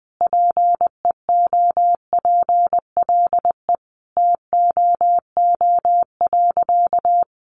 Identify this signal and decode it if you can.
Morse code, heard as a keyed tone, about 20 wpm, 700 Hz. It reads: PEOPLE TOO.
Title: Sentences from Top 1000 Words 20wpm